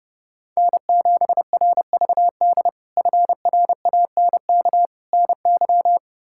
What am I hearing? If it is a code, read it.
Morse code, heard as a keyed tone, about 30 wpm, 700 Hz. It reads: N7RVD FRANK NY
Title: K1USN SST QSO Practice - 30wpm